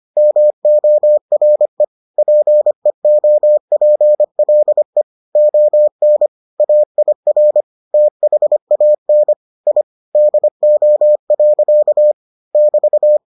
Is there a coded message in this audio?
Yes: Morse code from 0.2 to 13.2 s, heard as a keyed tone, about 25 wpm, 600 Hz. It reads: MORE PEOPLE ON AIR THAN I DO. <BT>